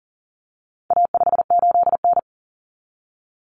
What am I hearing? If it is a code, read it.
Morse code, heard as a keyed tone, about 40 wpm, 700 Hz. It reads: A58N
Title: Common Cut Numbers - 4-Digit 40wpm